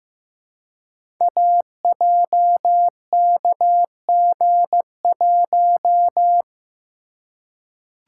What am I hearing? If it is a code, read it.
Morse code, heard as a keyed tone, about 15 wpm, 700 Hz. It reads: AJKG1
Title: Code Groups - 5 Letter:Numbers - 15wpm